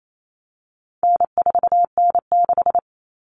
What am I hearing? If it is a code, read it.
Morse code, heard as a keyed tone, about 28 wpm, 700 Hz. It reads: N4N6
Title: Common Cut Numbers - 4-Digit 28wpm